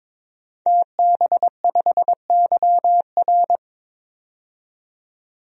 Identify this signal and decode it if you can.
Morse code, heard as a keyed tone, about 21 wpm, 700 Hz. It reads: TB5YR
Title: DXCC-100 Call Signs - 5 Characters - 22wpm